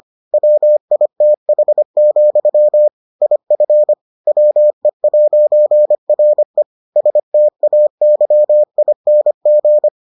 Morse code, heard as a keyed tone, about 25 wpm, 600 Hz. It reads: WITH, IF WE'RE STAYING